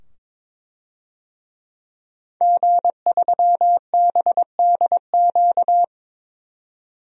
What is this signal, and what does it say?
Morse code, heard as a keyed tone, about 22 wpm, 700 Hz. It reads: G3BDQ
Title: DXCC-100 Call Signs - 5 Characters - 22wpm